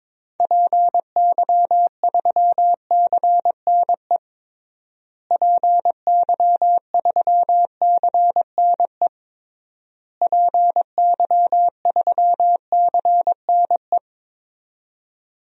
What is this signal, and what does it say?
Morse code, heard as a keyed tone, about 22 wpm, 700 Hz. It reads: PY3CNE PY3CNE PY3CNE
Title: DXCC-100 Call Signs - 6 Characters - Repeated 3x - 22wpm